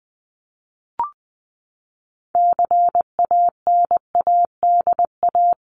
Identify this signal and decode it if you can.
Morse code, heard as a keyed tone, about 20 wpm, 700 Hz. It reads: CANADA
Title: Sets of 5 Words Encore 20wpm